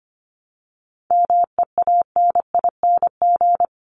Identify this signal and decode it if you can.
Morse code, heard as a keyed tone, about 25 wpm, 700 Hz. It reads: MEANING
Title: ING Suffix - Top 100 Words with Base Words 25wpm